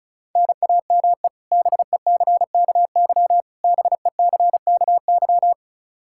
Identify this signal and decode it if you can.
Morse code, heard as a keyed tone, about 34 wpm, 700 Hz. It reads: NAME BECKY BECKY
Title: K1USN SST QSO Practice 35wpm